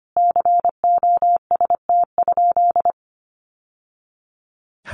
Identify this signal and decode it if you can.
Morse code, heard as a keyed tone, about 25 wpm, 700 Hz. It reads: COST?
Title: Sentences from Top 800 Words 25wpm